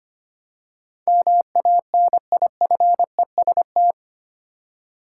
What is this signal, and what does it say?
Morse code, heard as a keyed tone, about 25 wpm, 700 Hz. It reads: MANIFEST
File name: 8 Letter Words - First 3 Letters Given - 25wpm